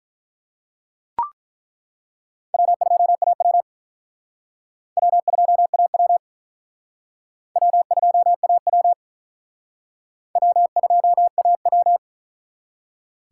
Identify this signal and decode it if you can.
Morse code, heard as a keyed tone, about 42 wpm, 700 Hz. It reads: W2AW W2AW W2AW W2AW
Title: Speed Racing - USA 1x2 Call Signs 35wpm